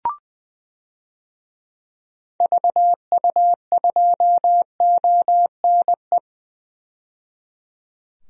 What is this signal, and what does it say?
Morse code, heard as a keyed tone, about 20 wpm, 700 Hz. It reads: VU2ONE
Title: DXCC-100 Call Signs - 6 Characters - 20wpm